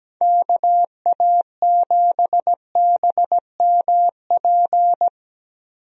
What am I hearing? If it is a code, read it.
Morse code, heard as a keyed tone, about 17 wpm, 700 Hz. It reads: KA7BMP
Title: USA Call Signs - 6 Characters - 17wpm